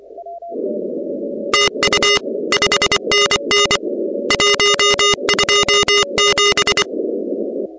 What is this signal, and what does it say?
Morse code, heard as a keyed tone, about 24 wpm, 2850 Hz, with noise. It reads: TU 5NN 127
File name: contests / 2019-cw-wpx-kt5j-k5pi- -2330 0-16